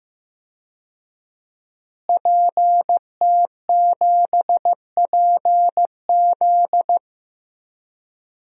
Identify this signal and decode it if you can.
Morse code, heard as a keyed tone, about 15 wpm, 700 Hz. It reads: PT7PZ
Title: DXCC-100 Call Signs - 5 Characters - Repeated 2x - 15wpm